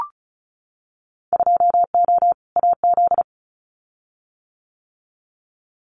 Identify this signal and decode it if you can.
Morse code, heard as a keyed tone, about 34 wpm, 690 Hz. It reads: 2O AZ
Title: Winter Field Day Exchanges 35wpm